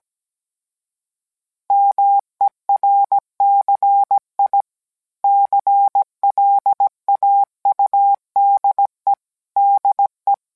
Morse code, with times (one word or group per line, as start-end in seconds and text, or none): 1.7-4.6 MERCI
5.2-9.1 CLAUDE
9.6-10.3 DE